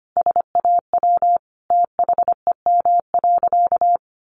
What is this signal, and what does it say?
Morse code, heard as a keyed tone, about 25 wpm, 700 Hz. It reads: SAW THEM.
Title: Sentences from Top 1300 Words 25wpm